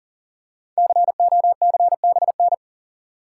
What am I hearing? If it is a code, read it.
Morse code, heard as a keyed tone, about 40 wpm, 700 Hz. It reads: COCBN
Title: Code Groups - 5 Letters - 40wpm